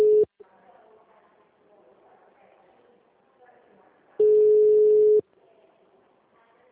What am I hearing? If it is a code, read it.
Ringback tone (Europe, 425 Hz), with speech.
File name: out- -2015- - - .41167